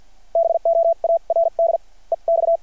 Morse code, heard as a keyed tone, about 48 wpm, spaced as for 40 wpm, 660 Hz. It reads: BOARD EX
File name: archived / CRSnet